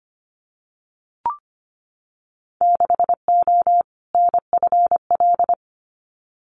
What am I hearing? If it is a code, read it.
Morse code, heard as a keyed tone, about 25 wpm, 700 Hz. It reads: 6O NFL
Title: Winter Field Day Exchanges 25wpm